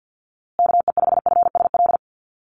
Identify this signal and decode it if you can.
Morse code, heard as a keyed tone, about 50 wpm, 700 Hz. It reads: KE5FSL
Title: USA Call Signs Repeated 3x 50wpm